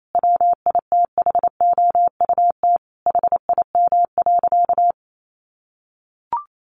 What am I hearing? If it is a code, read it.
Morse code, heard as a keyed tone, about 28 wpm, 700 Hz. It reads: WITHOUT HIM.
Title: Sentences from Top 1400 Words 28wpm